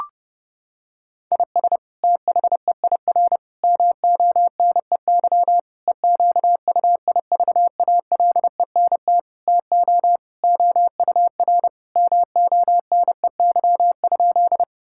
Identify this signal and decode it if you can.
Morse code, heard as a keyed tone, about 30 wpm, 700 Hz. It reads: IS THEIR MONEY EQUIVALENT TO OUR MONEY?
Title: Sentences from Top 1600 Words 30wpm